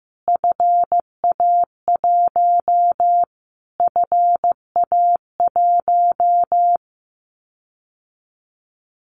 Morse code, heard as a keyed tone, about 15 wpm, 700 Hz. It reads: FA1 FA1
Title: Callsign Prefixes for Top 25 DXCC - Repeated 3x - 15wpm